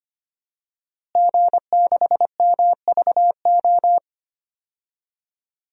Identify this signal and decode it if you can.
Morse code, heard as a keyed tone, about 25 wpm, 700 Hz. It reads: G6MVO
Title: DXCC-100 Call Signs - 5 Characters - Repeated 2x - 25wpm